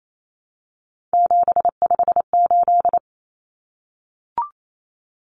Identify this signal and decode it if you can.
Morse code, heard as a keyed tone, about 28 wpm, 700 Hz. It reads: 758
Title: Numbers - 3-Digit 28wpm